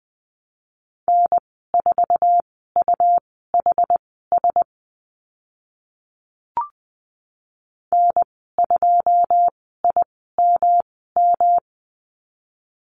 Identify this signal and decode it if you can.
Morse code, heard as a keyed tone, about 20 wpm, spaced as for 14 wpm, 700 Hz. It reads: N4UHS N2IMM
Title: 128 Lesson 21g - Number 6 - Call Signs 20wpm